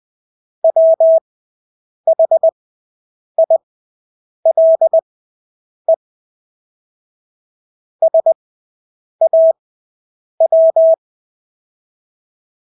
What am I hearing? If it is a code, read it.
Morse code, heard as a keyed tone, about 20 wpm, spaced as for 8 wpm, 650 Hz. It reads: WHILE SAW
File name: Engelske ord 20/8 wpm